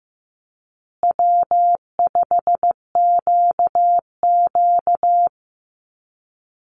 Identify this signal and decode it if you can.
Morse code, heard as a keyed tone, about 15 wpm, 700 Hz. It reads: W5QQ
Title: Call Signs Repeated 2x - 15wpm